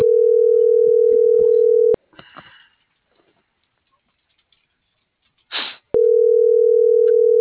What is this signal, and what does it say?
Ringback tone (North America), 440 + 480 Hz.